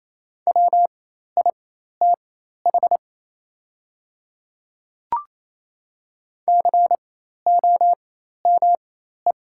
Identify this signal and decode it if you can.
Morse code, heard as a keyed tone, about 28 wpm, spaced as for 13 wpm, 700 Hz. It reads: WITH COME
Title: 4x Character Spacing - Top 100 Words - 28wpm